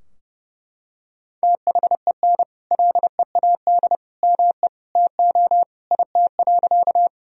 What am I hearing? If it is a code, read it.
Morse code, heard as a keyed tone, about 30 wpm, 700 Hz. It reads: THEN LEAD ME TO IT.
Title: Sentences from Top 1000 Words - Review Part 2 - 30wpm